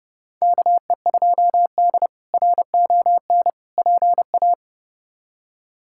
Morse code, heard as a keyed tone, about 30 wpm, 700 Hz. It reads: KE2D RON PA
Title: K1USN SST QSO Practice - 30wpm